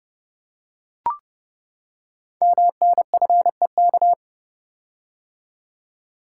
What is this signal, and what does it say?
Morse code, heard as a keyed tone, about 28 wpm, 700 Hz. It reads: MNFEK
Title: Code Groups - 5 Letters - 30wpm